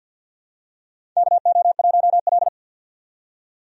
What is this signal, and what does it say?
Morse code, heard as a keyed tone, about 48 wpm, 700 Hz. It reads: KO1L
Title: USA Call Signs - 4 Characters - 50wpm